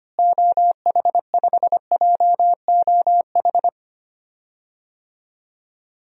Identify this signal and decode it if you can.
Morse code, heard as a keyed tone, about 25 wpm, 700 Hz. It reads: OH5JOH
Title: DXCC-100 Call Signs - 6 Characters - Repeated 2x - 25wpm